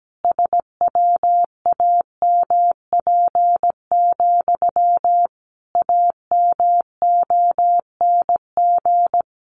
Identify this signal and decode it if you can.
Morse code, heard as a keyed tone, about 17 wpm, 700 Hz. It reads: SWAMP, AMONG
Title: A Sound Of Thunder 17wpm